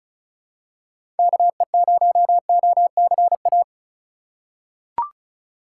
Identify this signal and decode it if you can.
Morse code, heard as a keyed tone, about 36 wpm, 700 Hz. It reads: KE0OCA
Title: USA Call Signs - 6 Characters - 6x Character Spacing - 35wpm